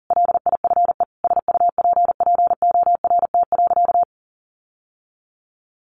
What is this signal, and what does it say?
Morse code, heard as a keyed tone, about 40 wpm, 700 Hz. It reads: LIFE SUPPORT.
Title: Sentences from Top 1300 Words 40wpm